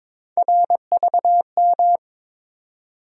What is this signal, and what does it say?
Morse code, heard as a keyed tone, about 22 wpm, 700 Hz. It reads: RVM
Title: USA Call Sign Postfixes - 3 Letters 22wpm